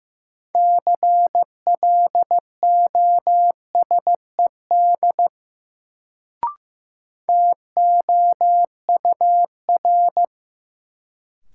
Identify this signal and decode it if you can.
Morse code, heard as a keyed tone, about 16 wpm, 700 Hz. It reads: CLOSED TOUR
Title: Sentences from Top 1400 Words - New Word Drill - 15wpm